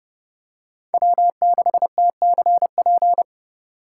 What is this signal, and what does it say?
Morse code, heard as a keyed tone, about 29 wpm, 700 Hz. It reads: W6TCP